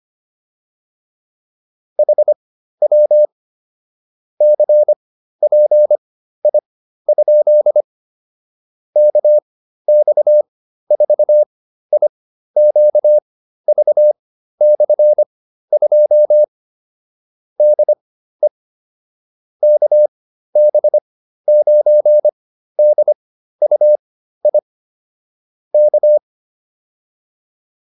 Morse code, heard as a keyed tone, about 25 wpm, spaced as for 13 wpm, 600 Hz. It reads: HW CPI? KX4IQV/2 DE KB9DUI K